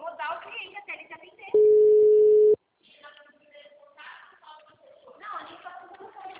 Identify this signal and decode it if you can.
Ringback tone (Europe, 425 Hz), with speech.